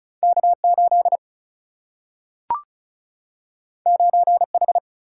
Morse code, heard as a keyed tone, about 36 wpm, 700 Hz. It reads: K 8 9 H
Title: Single-Letter Single-Number 35wpm